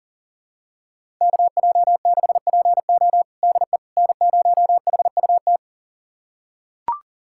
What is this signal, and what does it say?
Morse code, heard as a keyed tone, about 40 wpm, 700 Hz. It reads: KJ6PO DE N0HUT
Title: USA Callsign DE Callsign - 5 Characters - 40wpm